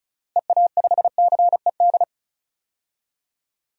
Morse code, heard as a keyed tone, about 36 wpm, 700 Hz. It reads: EA5CED
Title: DXCC-100 Call Signs - 6 Characters - Repeated 3x - 35wpm